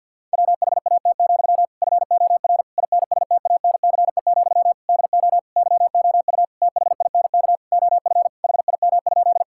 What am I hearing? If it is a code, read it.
Morse code, heard as a keyed tone, about 50 wpm, 700 Hz. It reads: WHAT, FOR INSTANCE, DO YOU THINK OF HIM?
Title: Sentences from Top 1000 Words - Review Part 5 - 50wpm